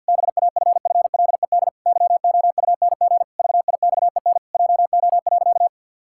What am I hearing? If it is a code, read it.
Morse code, heard as a keyed tone, about 51 wpm, 700 Hz. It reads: BAFFLED YOUNG VIXEN JO.
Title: Pangram Challenge v2 - 50wpm